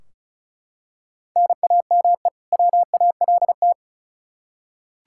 Morse code, heard as a keyed tone, about 36 wpm, 700 Hz. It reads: NAME WALT
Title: Practice QSO - Exchange 1 - Beginning - 35wpm